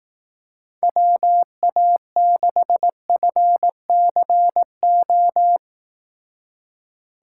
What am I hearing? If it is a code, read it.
Morse code, heard as a keyed tone, about 18 wpm, 700 Hz. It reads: WA6FCO
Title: Speed Racing - USA 2x3 Call Signs 15wpm